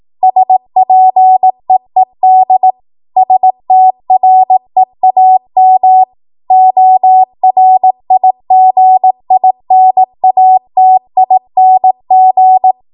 Morse code, heard as a keyed tone, about 18 wpm, 750 Hz. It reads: SPEED STREAM ORIGINATING
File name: Morse / 18WPM